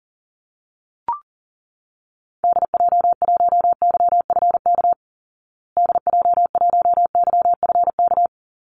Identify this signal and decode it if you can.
Morse code, heard as a keyed tone, about 40 wpm, 700 Hz. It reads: DJ1YFK DJ1YFK
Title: Call Signs Repeated 2x - 40wpm